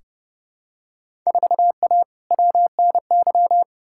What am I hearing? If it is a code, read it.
Morse code, heard as a keyed tone, about 29 wpm, 700 Hz. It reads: 4A WNY
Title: Field Day Exchanges Encore - 30wpm